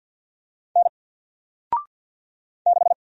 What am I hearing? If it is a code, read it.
Morse code, heard as a keyed tone, about 50 wpm, 700 Hz. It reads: N 6
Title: Single Letter-Number 50wpm